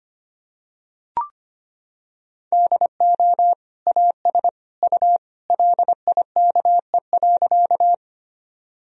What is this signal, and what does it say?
Morse code, heard as a keyed tone, about 25 wpm, 700 Hz. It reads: DO AS U LIKE.
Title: Sentences from Top 100 Words and Abbreviations 25wpm